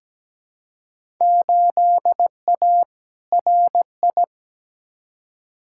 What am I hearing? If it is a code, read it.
Morse code, heard as a keyed tone, about 17 wpm, 700 Hz. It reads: 8A RI
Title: Field Day Exchanges Encore - 17wpm